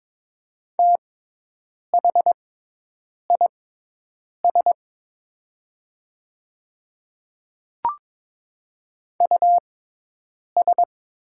Morse code, heard as a keyed tone, about 22 wpm, spaced as for 8 wpm, 700 Hz. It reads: THIS US
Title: 6x Character Spacing - Top 200 Words - 22wpm